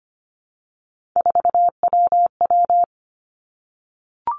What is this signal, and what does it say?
Morse code, heard as a keyed tone, about 25 wpm, 700 Hz. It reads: 4WW
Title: Three Challenging Letter-Number Combinations 25wpm